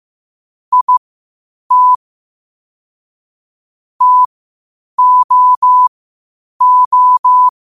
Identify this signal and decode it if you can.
Morse code, heard as a keyed tone, about 15 wpm, spaced as for 8 wpm, 1000 Hz. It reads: IT TOO
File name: qotdfiles / 8wpm